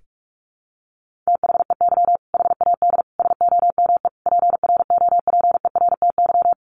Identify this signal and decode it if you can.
Morse code, heard as a keyed tone, about 44 wpm, 700 Hz. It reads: THEY HAD SOME PROPERTY
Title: Sentences from Top 1500 Words 45wpm